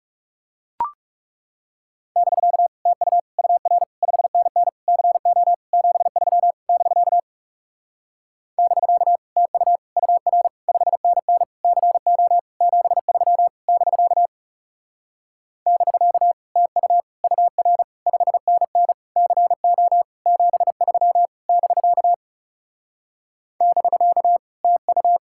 Morse code, heard as a keyed tone, about 40 wpm, 700 Hz. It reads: <BK> TU UR 5NN CO 73 <BK> <BK> TU UR 5NN CO 73 <BK> <BK> TU UR 5NN CO 73 <BK> <BK> TU